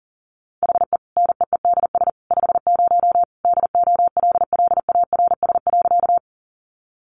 Morse code, heard as a keyed tone, about 38 wpm, 700 Hz. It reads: HE NEEDS 50 DOLLARS.
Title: Additional Easy Phrases 40wpm